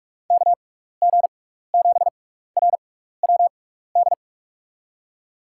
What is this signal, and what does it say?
Morse code, heard as a keyed tone, about 44 wpm, spaced as for 15 wpm, 700 Hz. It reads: KG7RWD
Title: USA Call Signs - 6 Characters - 6x Character Spacing - 45wpm